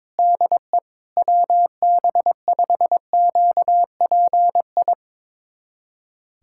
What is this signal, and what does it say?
Morse code, heard as a keyed tone, about 22 wpm, 700 Hz. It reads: DE WB5QPI
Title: USA Callsign DE Callsign - 6 Characters - 22wpm